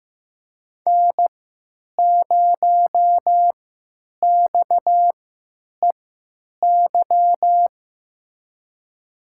Morse code, heard as a keyed tone, about 16 wpm, spaced as for 9 wpm, 700 Hz. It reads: N0XEY
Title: USA Call Signs - 5 Characters - 3x Character Spacing - 15wpm